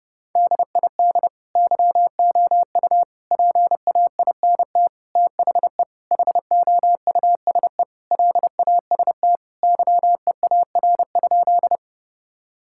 Morse code, heard as a keyed tone, about 30 wpm, 700 Hz. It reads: DID YOU PAINT THE HOUSE LAST YEAR?
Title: Sentences from Top 1000 Words - Review Part 6 - 30wpm